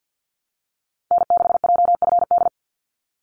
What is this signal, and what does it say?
Morse code, heard as a keyed tone, about 50 wpm, 700 Hz. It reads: N6JFD
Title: NA SOTA Activator Callsigns 50wpm